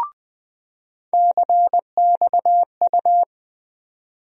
Morse code, heard as a keyed tone, about 20 wpm, 700 Hz. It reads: CXU
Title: USA Call Sign Postfixes - 3 Letters 20wpm